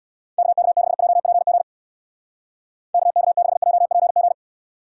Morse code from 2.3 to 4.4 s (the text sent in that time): KK6JPG